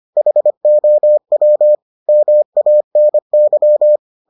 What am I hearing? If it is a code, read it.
Morse code, heard as a keyed tone, about 25 wpm, 600 Hz. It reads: HOW MANY